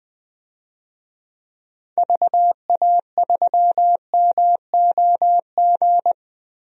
Morse code, heard as a keyed tone, about 21 wpm, 700 Hz. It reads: VA3MOG
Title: DXCC-100 Call Signs - 6 Characters - Repeated 2x - 20wpm